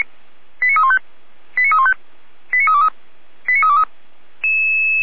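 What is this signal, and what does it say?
Five-tone selective call (ZVEI), heard as quick runs of five tones, tones of about 70 ms. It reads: 89315 89315 89323 89323
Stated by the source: ZVEI five-tone selective call 89320, received on an SDR.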